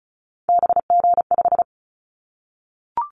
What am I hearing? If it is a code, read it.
Morse code, heard as a keyed tone, about 36 wpm, 700 Hz. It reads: BG5
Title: Callsign Prefixes for Top 100 DXCC - 35wpm